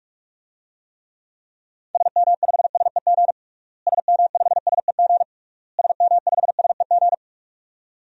Morse code, heard as a keyed tone, about 44 wpm, 700 Hz. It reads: SM5SEG SM5SEG SM5SEG
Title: DXCC-100 Call Signs - 6 Characters - Repeated 3x - 45wpm